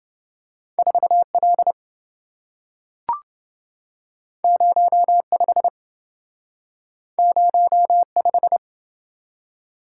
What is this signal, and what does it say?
Morse code, heard as a keyed tone, about 27 wpm, 700 Hz. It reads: 4L 05 05